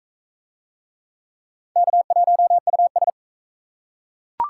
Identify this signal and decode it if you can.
Morse code, heard as a keyed tone, about 40 wpm, 700 Hz. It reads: K1US